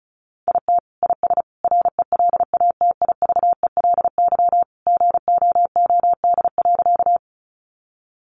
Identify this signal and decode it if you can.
Morse code, heard as a keyed tone, about 34 wpm, 700 Hz. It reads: IT IS RELATIVELY GOOD.
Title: Sentences from Top 1100 Words 35wpm